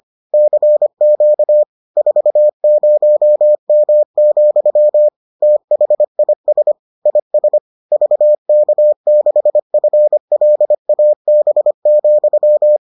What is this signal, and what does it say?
Morse code, heard as a keyed tone, about 25 wpm, 600 Hz. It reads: CQ 40M, THIS IS VK6FLAB,